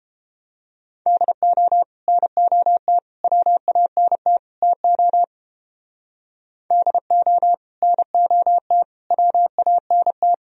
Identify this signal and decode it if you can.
Morse code, heard as a keyed tone, about 31 wpm, 700 Hz. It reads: DO NOT WANT TO DO NOT WANT